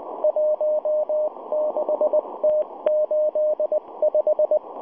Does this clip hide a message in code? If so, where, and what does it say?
Yes: Morse code from 0.2 to 4.6 s, heard as a keyed tone, about 19 wpm, 610 Hz, with radio noise. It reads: 16T85